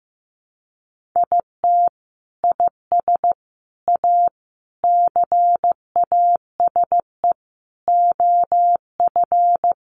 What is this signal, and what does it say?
Morse code, heard as a keyed tone, about 15 wpm, 700 Hz. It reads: IT IS A CASE OF
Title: Sentences from Top 1400 Words 15wpm